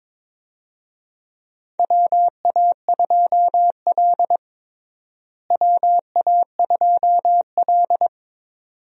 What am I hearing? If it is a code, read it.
Morse code, heard as a keyed tone, about 21 wpm, 700 Hz. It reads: WA2L WA2L
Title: USA Call Signs - 4 Characters - Repeated 2x - 22wpm